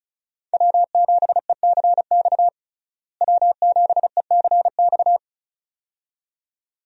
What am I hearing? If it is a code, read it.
Morse code, heard as a keyed tone, about 36 wpm, 700 Hz. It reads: W7ECX W7ECX
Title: USA Call Signs - 5 Characters - Repeated 2x - 35wpm